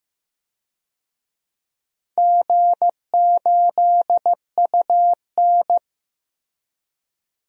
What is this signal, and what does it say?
Morse code, heard as a keyed tone, about 15 wpm, 700 Hz. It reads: G8UN
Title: DXCC-100 Call Signs - 4 Characters - 15wpm